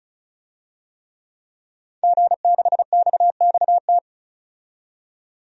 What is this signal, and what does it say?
Morse code, heard as a keyed tone, about 36 wpm, 700 Hz. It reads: G6XXT
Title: DXCC-100 Call Signs - 5 Characters - 35wpm